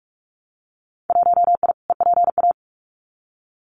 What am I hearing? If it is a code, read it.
Morse code, heard as a keyed tone, about 46 wpm, 700 Hz. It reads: 1I EPA
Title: Winter Field Day Exchanges Encore 45wpm